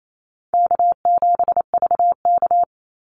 Morse code, heard as a keyed tone, about 28 wpm, 700 Hz. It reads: K7VK
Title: NA SOTA Activator Callsigns 28wpm